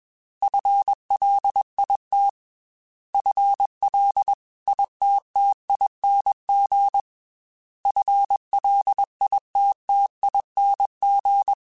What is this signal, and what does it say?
Morse code, heard as a keyed tone, about 21 wpm, 800 Hz. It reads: FLIT FLITTING FLITTING
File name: FL / FL mp3s / FL Prefixes 3x2 sp @20